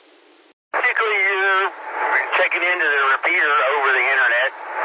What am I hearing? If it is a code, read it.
Emergency Alert System (SAME) tones, heard as harsh buzzing bursts, with speech. It is heard from 1.1 to 4.4 s.